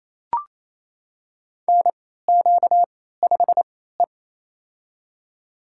Morse code, heard as a keyed tone, about 28 wpm, spaced as for 16 wpm, 700 Hz. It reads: NQ5E
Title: USA Call Signs - 3x Character Spacing - 28wpm